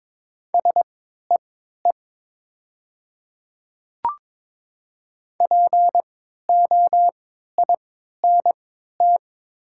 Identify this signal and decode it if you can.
Morse code, heard as a keyed tone, about 22 wpm, spaced as for 12 wpm, 700 Hz. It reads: SEE POINT
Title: 3x Character Spacing - Top 500 Words - 22wpm